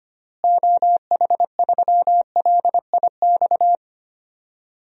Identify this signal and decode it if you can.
Morse code, heard as a keyed tone, about 25 wpm, 700 Hz. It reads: OH3LIX
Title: DXCC-100 Call Signs - 6 Characters - Repeated 2x - 25wpm